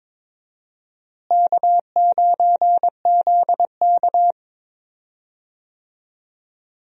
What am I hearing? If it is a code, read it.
Morse code, heard as a keyed tone, about 22 wpm, 700 Hz. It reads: K9ZK
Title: USA Call Signs - 4 Characters - Repeated 2x - 22wpm